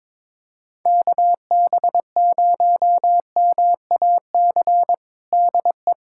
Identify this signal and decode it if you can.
Morse code, heard as a keyed tone, about 22 wpm, 700 Hz. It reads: KB0MAC DE
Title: USA Callsign DE Callsign - 6 Characters - 22wpm